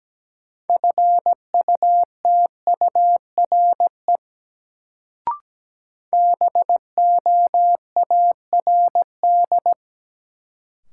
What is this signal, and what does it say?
Morse code, heard as a keyed tone, about 17 wpm, 700 Hz. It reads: FUTURE BOARD
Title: Sentences from Top 700 Words - New Word Drill - 17wpm